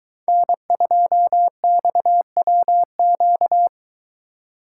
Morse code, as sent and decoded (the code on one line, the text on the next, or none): -. ..--- -..- .-- --.-
N2XWQ